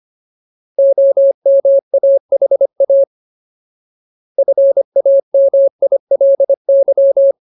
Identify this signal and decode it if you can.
Morse code, heard as a keyed tone, about 25 wpm, 550 Hz. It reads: OMAHA FAMILY
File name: Ultimate Word List / 25WPM Ultimate